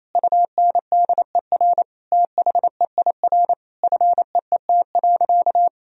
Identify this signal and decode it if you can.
Morse code, heard as a keyed tone, about 28 wpm, 700 Hz. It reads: UNDER THEIR FEET.
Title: Sentences from Top 1000 Words - Review Part 4 - 28wpm